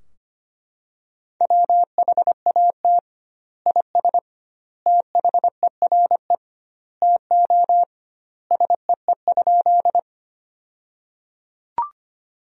Morse code, heard as a keyed tone, about 25 wpm, 700 Hz. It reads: WHAT IS THERE TO SEE?
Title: 2x Word-Spacing - Sentences from Top 100 Words - 25wpm